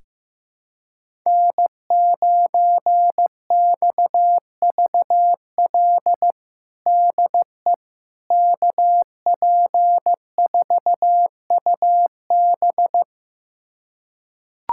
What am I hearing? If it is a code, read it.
Morse code, heard as a keyed tone, about 16 wpm, 700 Hz. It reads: N9XVL DE KP4UB
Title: USA Callsign DE Callsign 15wpm